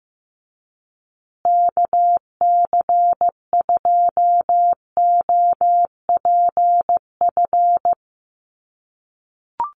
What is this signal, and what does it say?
Morse code, heard as a keyed tone, about 15 wpm, 700 Hz. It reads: KC2OPF
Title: USA Call Signs Repeated 2x 15wpm